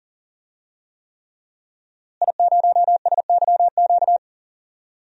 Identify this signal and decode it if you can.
Morse code, heard as a keyed tone, about 37 wpm, 700 Hz. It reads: I0SYQ
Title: DXCC-100 Call Signs - 5 Characters - Repeated 3x - 40wpm